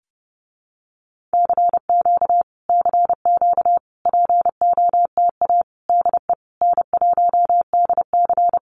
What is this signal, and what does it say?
Morse code, heard as a keyed tone, about 30 wpm, 700 Hz. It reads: CQ CQ POTA DE N1DC
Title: Common QSO Phrases Encore 30wpm